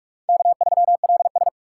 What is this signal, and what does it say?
Morse code, heard as a keyed tone, about 46 wpm, 700 Hz. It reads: K3LS